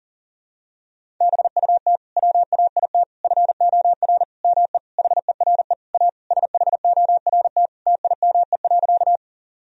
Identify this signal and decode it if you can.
Morse code, heard as a keyed tone, about 37 wpm, 700 Hz. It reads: BUT WAIT FOR ME HERE A SHORT TIME.
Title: Sentences from Top 1000 Words - Review Part 2 - 40wpm